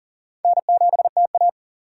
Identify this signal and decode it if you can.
Morse code, heard as a keyed tone, about 40 wpm, 700 Hz. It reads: N7TA
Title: USA Call Signs - 2x Character Spacing - 40wpm